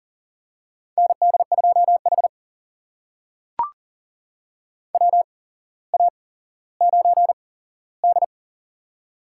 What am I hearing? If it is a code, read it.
Morse code, heard as a keyed tone, about 42 wpm, spaced as for 17 wpm, 700 Hz. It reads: ND2H WA9D
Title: USA Call Signs - 4 Characters - 8x Character Spacing - 40wpm